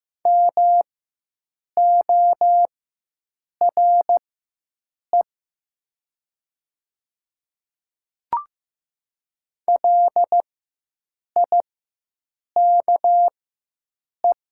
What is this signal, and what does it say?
Morse code, heard as a keyed tone, about 15 wpm, spaced as for 7 wpm, 700 Hz. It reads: MORE LIKE
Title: 4x Character Spacing - Top 100 Words - 15wpm